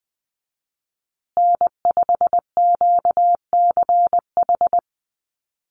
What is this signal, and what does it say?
Morse code, heard as a keyed tone, about 20 wpm, 700 Hz. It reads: N5QCH
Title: USA Call Signs Repeated 3x 20wpm